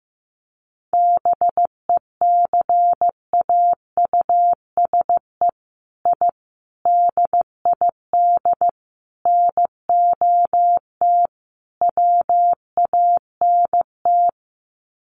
Morse code, heard as a keyed tone, about 15 wpm, 700 Hz. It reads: BECAUSE I DID NOT WANT
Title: Sets of 5 Words Encore2 15wpm